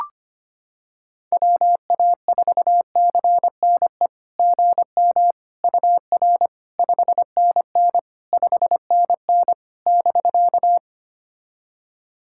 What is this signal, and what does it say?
Morse code, heard as a keyed tone, about 25 wpm, 690 Hz. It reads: WA4CNE GM UR 5NN 5NN <BK>
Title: SOTA QSO Practice 25wpm